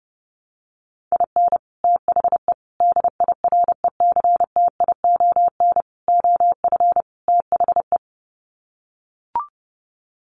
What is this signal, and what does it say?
Morse code, heard as a keyed tone, about 28 wpm, 700 Hz. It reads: IN THE DIRECTION OF THE
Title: Sets of 5 Words 30wpm